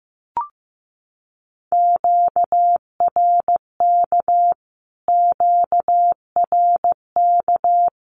Morse code, heard as a keyed tone, about 14 wpm, 700 Hz. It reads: QRK QRK
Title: Q-Codes Repeated 3x 15wpm